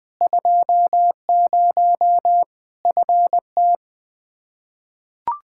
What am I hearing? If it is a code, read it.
Morse code, heard as a keyed tone, about 20 wpm, 700 Hz. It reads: 20 FT
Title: K1USN SST QSO Practice 20wpm